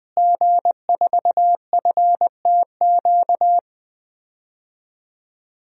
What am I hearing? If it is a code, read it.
Morse code, heard as a keyed tone, about 20 wpm, 700 Hz. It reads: G4FTQ
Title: DXCC-100 Call Signs - 5 Characters - Repeated 2x - 20wpm